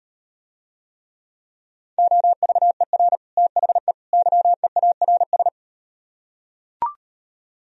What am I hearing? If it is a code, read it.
Morse code, heard as a keyed tone, about 38 wpm, 700 Hz. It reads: OVER THE YEARS